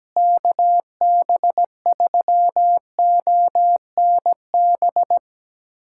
Morse code, heard as a keyed tone, about 17 wpm, 700 Hz. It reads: KB3ONB
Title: USA Callsign DE Callsign - 6 Characters - 17wpm